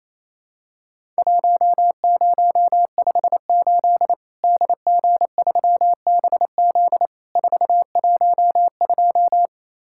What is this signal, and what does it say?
Morse code, heard as a keyed tone, about 28 wpm, 700 Hz. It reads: 1058 DG3BZ 412
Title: Worked All Europe DX Contest - QTC Exchange - 28wpm